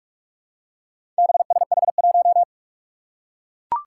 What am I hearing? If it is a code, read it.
Morse code, heard as a keyed tone, about 42 wpm, 700 Hz. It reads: BSH1
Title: Four Challenging Letter-Number Combinations 45wpm